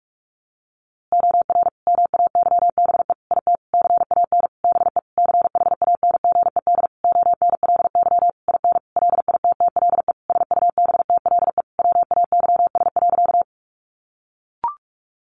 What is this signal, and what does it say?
Morse code, heard as a keyed tone, about 44 wpm, 700 Hz. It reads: OR MAYBE IT CAN BE CHANGED ONLY IN LITTLE SUBTLE WAYS.
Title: A Sound Of Thunder 45wpm